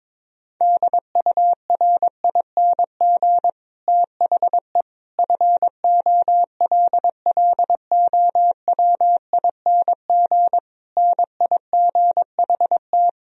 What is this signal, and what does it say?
Morse code, heard as a keyed tone, about 22 wpm, 700 Hz. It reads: DURING THE FOLLOWING NIGHT
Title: Sentences from Top 1000 Words - Review Part 4 - 22wpm